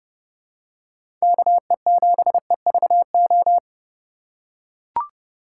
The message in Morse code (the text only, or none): KE7EVO